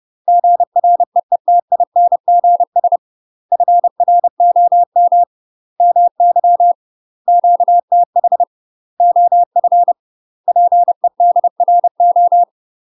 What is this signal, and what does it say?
Morse code, heard as a keyed tone, about 30 wpm, 700 Hz. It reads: GREETINGS FROM MY QTH OF PEDRO
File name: UK MASTER QSOs TRAINING / 30WPM / UK MASTER QSOs 0007